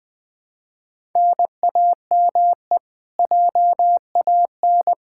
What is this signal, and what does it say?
Morse code, heard as a keyed tone, about 20 wpm, 700 Hz. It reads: NAME JAN
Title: Practice QSO - Exchange 1 - Beginning - 20wpm